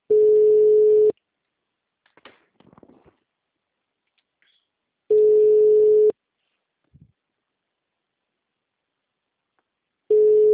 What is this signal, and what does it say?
Ringback tone (Europe, 425 Hz).
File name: out- -2036- - - .51170